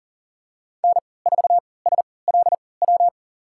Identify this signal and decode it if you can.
Morse code, heard as a keyed tone, about 40 wpm, spaced as for 23 wpm, 700 Hz. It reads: N4SLW
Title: USA Call Signs - 5 Characters - 3x Character Spacing - 40wpm